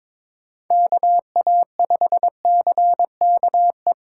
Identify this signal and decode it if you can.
Morse code, heard as a keyed tone, about 26 wpm, 700 Hz. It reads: KA5CKE
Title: Speed Racing - USA Call Signs 22wpm